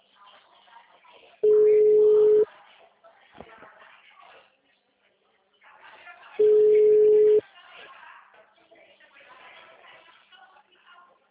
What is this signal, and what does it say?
Ringback tone (Europe, 425 Hz), with speech.